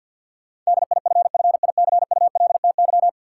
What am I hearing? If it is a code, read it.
Morse code, heard as a keyed tone, about 52 wpm, 700 Hz. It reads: DIFFICULTY
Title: Sentences from Top 1500 Words - New Word Drill Alt - 50wpm